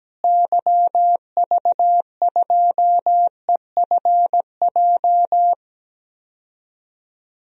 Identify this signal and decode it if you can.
Morse code, heard as a keyed tone, about 17 wpm, 700 Hz. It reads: YV2EFJ
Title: DXCC-100 Call Signs - 6 Characters - Repeated 2x - 17wpm